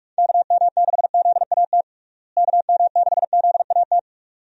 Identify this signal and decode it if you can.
Morse code, heard as a keyed tone, about 46 wpm, 700 Hz. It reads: KM6ZAT KM6ZAT
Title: USA Call Signs - 6 Characters - Repeated 2x - 45wpm